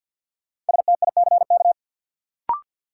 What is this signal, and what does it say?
Morse code, heard as a keyed tone, about 53 wpm, 700 Hz. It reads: STICK
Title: Sentences from Top 1200 Words - New Word Drill Alt - 50wpm